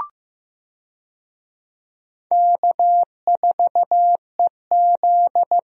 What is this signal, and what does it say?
Morse code, heard as a keyed tone, about 15 wpm, 700 Hz. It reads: K4EZ
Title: USA Call Signs - 4 Characters - Repeated 3x - 15wpm